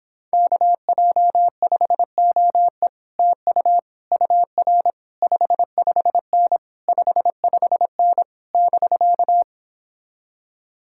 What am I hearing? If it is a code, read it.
Morse code, heard as a keyed tone, about 26 wpm, 700 Hz. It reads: KJ5OE TU UR 55N 55N <BK>